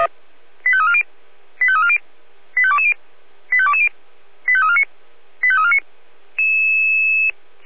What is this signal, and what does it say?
Five-tone selective call (ZVEI), heard as quick runs of five tones, tones of about 70 ms. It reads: 85340 85340 85220 85220 85349 85349